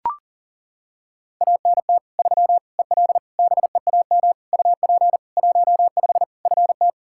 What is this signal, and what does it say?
Morse code, heard as a keyed tone, about 40 wpm, 700 Hz. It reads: ANT 3 EL BEAM UP 15 FT
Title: Practice QSO - Exchange 2 - Info - 40wpm